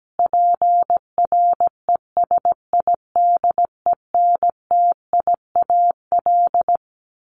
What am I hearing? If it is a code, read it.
Morse code, heard as a keyed tone, about 17 wpm, 700 Hz. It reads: PRESIDENTIAL
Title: A Sound Of Thunder 17wpm